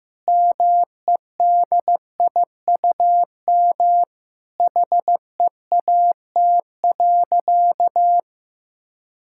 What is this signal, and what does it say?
Morse code, heard as a keyed tone, about 15 wpm, 700 Hz. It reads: MEDIUM HEAT.
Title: Sentences from Top 1000 Words - Review Part 6 - 15wpm